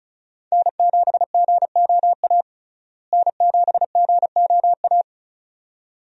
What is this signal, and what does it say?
Morse code, heard as a keyed tone, about 36 wpm, 700 Hz. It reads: N7GOA N7GOA
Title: USA Call Signs - 5 Characters - Repeated 2x - 35wpm